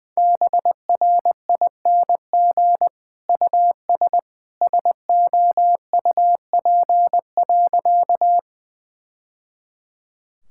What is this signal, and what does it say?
Morse code, heard as a keyed tone, about 20 wpm, 700 Hz. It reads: BRING US SOUP.
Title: Sentences from Top 1000 Words - Review Part 6 - 20wpm